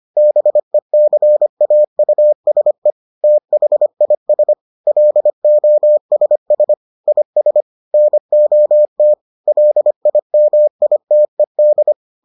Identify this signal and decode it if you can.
Morse code, heard as a keyed tone, about 25 wpm, 600 Hz. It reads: BECAUSE THIS LOSS IS NOT LIMITED